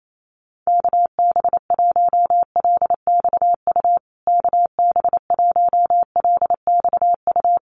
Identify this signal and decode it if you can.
Morse code, heard as a keyed tone, about 28 wpm, 700 Hz. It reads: KB1LXU KB1LXU
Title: USA Call Signs Repeated 3x 28wpm